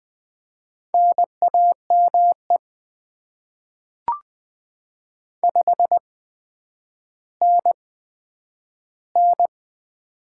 Morse code, heard as a keyed tone, about 20 wpm, spaced as for 10 wpm, 700 Hz. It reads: NAME 5NN
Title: Top 20 QSO Elements - 8x Character Spacing - 20wpm